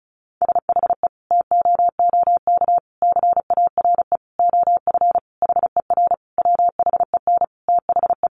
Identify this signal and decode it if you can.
Morse code, heard as a keyed tone, about 34 wpm, 700 Hz. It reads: SHE TOOK CARE OF HER WHEN THE
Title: Sentences from Top 1300 Words 35wpm